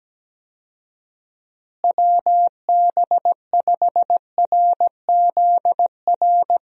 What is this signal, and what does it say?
Morse code, heard as a keyed tone, about 17 wpm, 700 Hz. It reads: WB5RZR
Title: USA Call Signs - 6 Characters - Repeated 3x - 17wpm